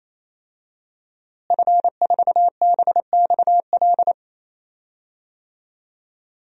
Morse code, heard as a keyed tone, about 28 wpm, 700 Hz. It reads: F4BXL
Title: DXCC-100 Call Signs - 5 Characters - Repeated 3x - 28wpm